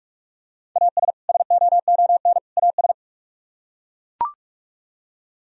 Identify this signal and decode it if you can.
Morse code, heard as a keyed tone, about 42 wpm, 700 Hz. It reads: AS SOON AS